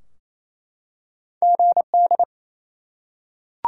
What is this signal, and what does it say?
Morse code, heard as a keyed tone, about 28 wpm, 700 Hz. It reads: GD
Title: Top 90 QSO Elements 28wpm